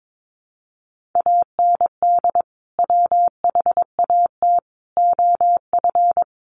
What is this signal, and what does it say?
Morse code, heard as a keyed tone, about 22 wpm, 700 Hz. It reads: AND WHAT OF
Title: Sentences from Top 900 Words 22wpm